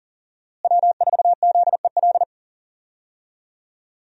Morse code, heard as a keyed tone, about 37 wpm, 700 Hz. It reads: W4ZEL
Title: USA Call Signs - 5 Characters - Repeated 2x - 40wpm